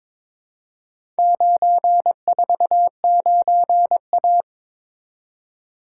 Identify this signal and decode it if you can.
Morse code, heard as a keyed tone, about 22 wpm, 700 Hz. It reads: 949A
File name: practice / 22wpm / Four Letter-Number Combinations - 22wpm